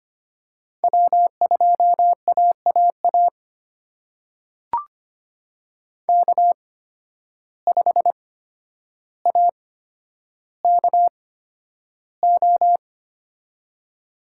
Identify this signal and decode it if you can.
Morse code, heard as a keyed tone, about 25 wpm, spaced as for 11 wpm, 700 Hz. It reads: W2AAA K5AKO
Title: USA Call Signs - 5 Characters - 8x Character Spacing - 25wpm